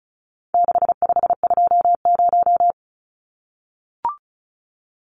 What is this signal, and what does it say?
Morse code, heard as a keyed tone, about 34 wpm, 700 Hz. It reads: 6520
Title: Numbers - 4-Digit 35wpm